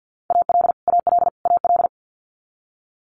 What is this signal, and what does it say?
Morse code, heard as a keyed tone, about 50 wpm, 700 Hz. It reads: AL AL AL
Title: Multi-letter Phonemes Repeated 3x 50wpm